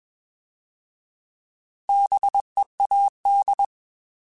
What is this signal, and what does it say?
Morse code, heard as a keyed tone, about 21 wpm, 800 Hz. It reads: BEAD
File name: Bs @20